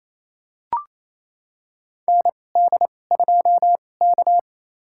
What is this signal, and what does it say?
Morse code, heard as a keyed tone, about 28 wpm, spaced as for 20 wpm, 700 Hz. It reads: ND2K
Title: USA Call Signs - 2x Character Spacing - 28wpm